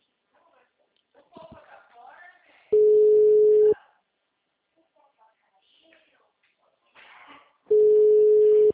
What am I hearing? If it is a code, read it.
Ringback tone (Europe, 425 Hz), with speech.
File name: out- -2008- - - .3186